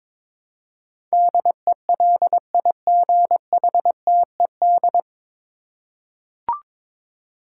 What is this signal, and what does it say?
Morse code, heard as a keyed tone, about 22 wpm, 700 Hz. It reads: DELIGHTED
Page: Morse Code Ninja practice 22wpm